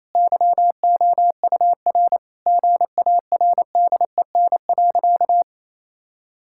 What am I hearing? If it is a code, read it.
Morse code, heard as a keyed tone, about 28 wpm, 700 Hz. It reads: YOUR GARDEN.
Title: Sentences from Top 1000 Words - Review Part 5 - 28wpm